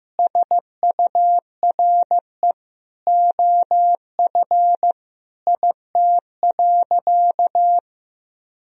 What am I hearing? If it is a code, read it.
Morse code, heard as a keyed tone, about 15 wpm, 700 Hz. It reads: SURE OF IT.
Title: Sentences from Top 1000 Words - Review Part 4 - 15wpm